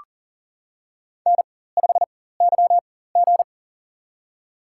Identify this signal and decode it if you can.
Morse code, heard as a keyed tone, about 40 wpm, spaced as for 19 wpm, 700 Hz. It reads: N5YG
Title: USA Call Signs - 4x Character Spacing - 40wpm